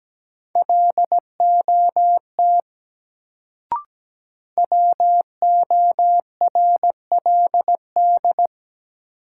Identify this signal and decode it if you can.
Morse code, heard as a keyed tone, about 17 wpm, 700 Hz. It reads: LOT WORLD
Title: Sets of 2 Words - New Word Drill - 17wpm